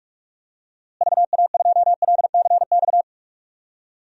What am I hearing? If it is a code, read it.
Morse code, heard as a keyed tone, about 46 wpm, 700 Hz. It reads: VA2LCX
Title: DXCC-100 Call Signs - 6 Characters - Repeated 2x - 45wpm